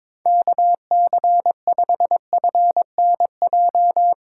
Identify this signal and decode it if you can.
Morse code, heard as a keyed tone, about 22 wpm, 700 Hz. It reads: KC5FNJ
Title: USA Call Signs - 6 Characters - 6x Character Spacing - 22wpm